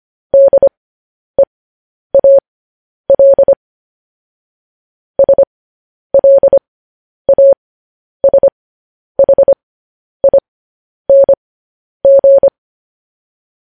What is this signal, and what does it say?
Morse code, heard as a keyed tone, about 25 wpm, spaced as for 10 wpm, 550 Hz. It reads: DEAL SLASHING
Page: News Headlines in Morse Code at 10 WPM